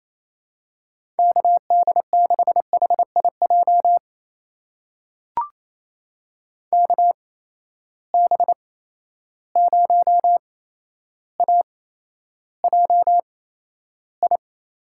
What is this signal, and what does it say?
Morse code, heard as a keyed tone, about 28 wpm, spaced as for 12 wpm, 700 Hz. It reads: KD6HIJ KB0AJI
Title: USA Call Signs - 6 Characters - 8x Character Spacing - 28wpm